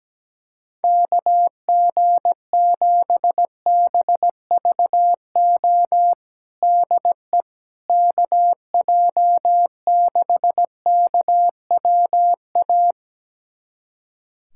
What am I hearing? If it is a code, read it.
Morse code, heard as a keyed tone, about 17 wpm, 700 Hz. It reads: KG7BVO DE KJ6KWA
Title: USA Callsign DE Callsign - 6 Characters - 17wpm